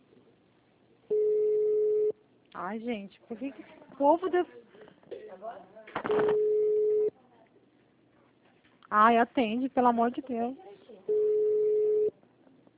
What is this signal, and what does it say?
Ringback tone (Europe, 425 Hz), with speech.
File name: out- -2002- - - .1043